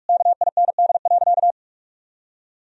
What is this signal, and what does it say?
Morse code, heard as a keyed tone, about 46 wpm, 700 Hz. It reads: KIND.
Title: Sentences from Top 1000 Words - Review Part 4 - 45wpm